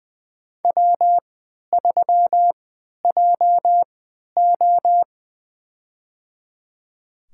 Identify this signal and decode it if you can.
Morse code, heard as a keyed tone, about 21 wpm, spaced as for 12 wpm, 700 Hz. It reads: W3JO
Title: USA Call Signs - 3x Character Spacing - 20wpm